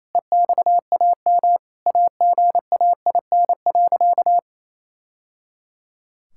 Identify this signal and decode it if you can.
Morse code, heard as a keyed tone, about 28 wpm, 700 Hz. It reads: EXAM AGAIN.
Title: Sentences from Top 1000 Words - Review Part 5 - 28wpm